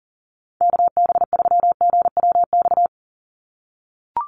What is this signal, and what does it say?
Morse code, heard as a keyed tone, about 40 wpm, 700 Hz. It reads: KB3GWX
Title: USA Call Signs 40wpm